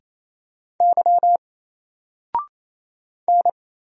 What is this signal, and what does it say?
Morse code, heard as a keyed tone, about 27 wpm, 700 Hz. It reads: Y N
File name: Single-Letter-Number-28wpm